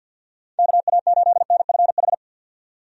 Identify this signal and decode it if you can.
Morse code, heard as a keyed tone, about 47 wpm, 700 Hz. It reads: KA8NFH
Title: USA Call Signs - 6 Characters - 50wpm